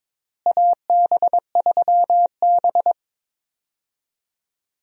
Morse code, heard as a keyed tone, about 22 wpm, 700 Hz. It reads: AB3B
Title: USA Call Signs - 4 Characters - Repeated 3x - 22wpm